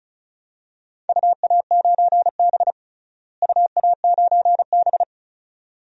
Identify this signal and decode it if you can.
Morse code, heard as a keyed tone, about 34 wpm, 700 Hz. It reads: UA9B UA9B
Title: DXCC-100 Call Signs - 4 Characters - Repeated 2x - 35wpm